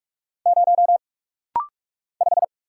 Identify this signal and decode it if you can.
Morse code, heard as a keyed tone, about 45 wpm, 700 Hz. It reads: 0 5
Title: Single Letter-Number Alt - Rapid-Fire 45wpm